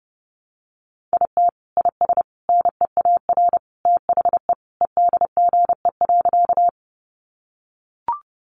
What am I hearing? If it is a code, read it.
Morse code, heard as a keyed tone, about 30 wpm, 700 Hz. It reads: IT IS NEAR THE EDGE.
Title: Additional Easy Phrases 30wpm